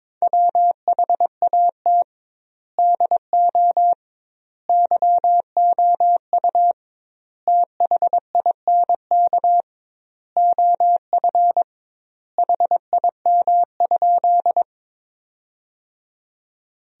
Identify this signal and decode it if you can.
Morse code, heard as a keyed tone, about 22 wpm, 700 Hz. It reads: WHAT DO YOU THINK OF HIM?
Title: 2x Word-Spacing - Sentences from Top 200 Words - 22wpm